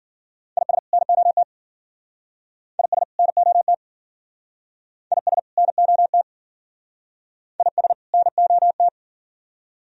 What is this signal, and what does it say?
Morse code, heard as a keyed tone, about 51 wpm, 690 Hz. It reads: IS NOT IS NOT IS NOT IS NOT